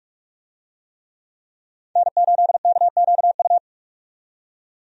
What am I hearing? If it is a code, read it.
Morse code, heard as a keyed tone, about 46 wpm, 700 Hz. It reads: N8KQU
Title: USA Call Signs - 5 Characters - 45wpm